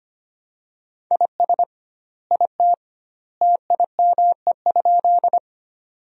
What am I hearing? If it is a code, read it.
Morse code, heard as a keyed tone, about 25 wpm, 700 Hz. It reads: IS IT TIME?
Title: 2x Word-Spacing - Sentences from Top 100 Words - 25wpm